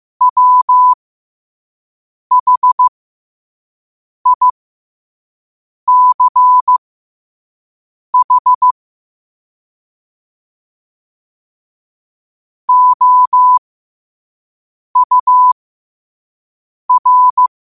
Morse code, heard as a keyed tone, about 15 wpm, spaced as for 5 wpm, 1000 Hz. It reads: WHICH OUR
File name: qotdfiles / 5wpm